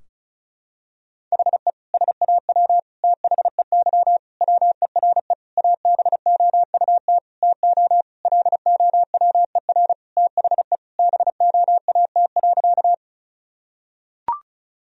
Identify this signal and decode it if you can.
Morse code, heard as a keyed tone, about 36 wpm, 700 Hz. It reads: HE SAW THEY WERE ABOUT TO LOWER THE BOAT.
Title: Sentences from Top 1000 Words - Review Part 4 - 35wpm